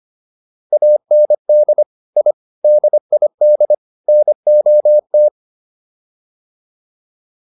Morse code, heard as a keyed tone, about 25 wpm, 600 Hz. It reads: AND I DID NOT